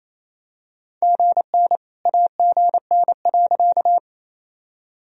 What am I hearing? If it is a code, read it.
Morse code, heard as a keyed tone, about 28 wpm, 700 Hz. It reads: GN AGN.
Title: Sentences from Top 300 Words and Abbreviations 28wpm